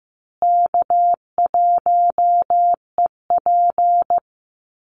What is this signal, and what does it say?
Morse code, heard as a keyed tone, about 15 wpm, 700 Hz. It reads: K1EP
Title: Call Signs Repeated 3x - 15wpm